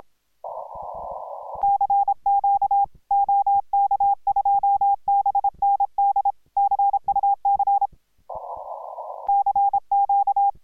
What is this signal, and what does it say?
Morse code, heard as a keyed tone, about 26 wpm, 810 Hz. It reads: CQ OK2BND CUC CQ